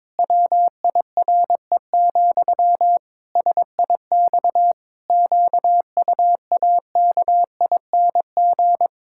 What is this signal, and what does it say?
Morse code, heard as a keyed tone, about 22 wpm, 700 Hz. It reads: WIRE, SIX QUAKING
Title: Pangram Challenge v2 - 22wpm